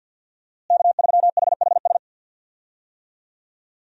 Morse code, heard as a keyed tone, about 47 wpm, 700 Hz. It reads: K3HHS
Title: USA Call Signs - 5 Characters - Repeated 3x - 50wpm